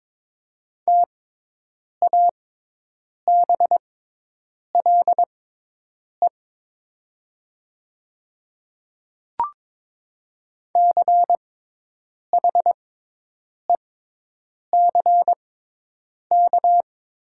Morse code, heard as a keyed tone, about 22 wpm, spaced as for 8 wpm, 700 Hz. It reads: TABLE CHECK
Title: 6x Character Spacing - Top 500 Words - 22wpm